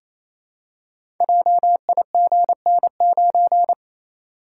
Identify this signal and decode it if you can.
Morse code, heard as a keyed tone, about 28 wpm, 700 Hz. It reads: JIGN9
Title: Code Groups - 5 Letter:Numbers - 28wpm